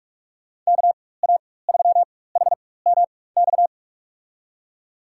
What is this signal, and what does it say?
Morse code, heard as a keyed tone, about 41 wpm, spaced as for 20 wpm, 700 Hz. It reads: KA3HMX
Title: USA Call Signs - 6 Characters - 4x Character Spacing - 45wpm